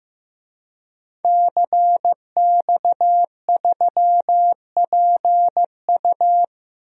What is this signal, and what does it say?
Morse code, heard as a keyed tone, about 15 wpm, 700 Hz. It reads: CX3PU
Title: DXCC-100 Call Signs - 5 Characters - Repeated 2x - 15wpm